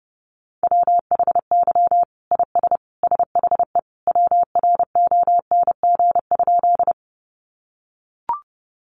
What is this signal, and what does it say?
Morse code, heard as a keyed tone, about 28 wpm, 700 Hz. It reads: WHY IS SHE WRONG?
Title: Sentences from Top 900 Words 30wpm